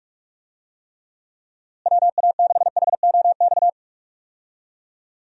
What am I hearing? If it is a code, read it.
Morse code, heard as a keyed tone, about 46 wpm, 700 Hz. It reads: WA6HOX
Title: USA Call Signs - 6 Characters - Repeated 2x - 45wpm